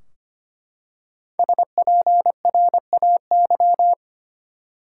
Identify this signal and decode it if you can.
Morse code, heard as a keyed tone, about 25 wpm, 700 Hz. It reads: SPRAY
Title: Five Letter Words - 25wpm